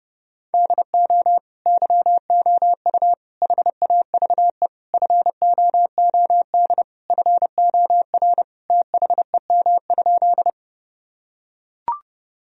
Morse code, heard as a keyed tone, about 30 wpm, 700 Hz. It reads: DO YOU HAVE FOOD FOR THEM?
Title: Sentences from Top 1000 Words - Review Part 1 - 30wpm